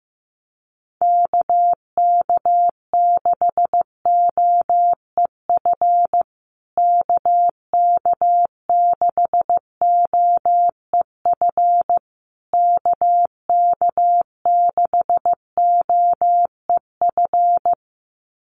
Morse code, heard as a keyed tone, about 15 wpm, 700 Hz. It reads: KK6OEF KK6OEF KK6OEF
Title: USA Call Signs Repeated 3x 15wpm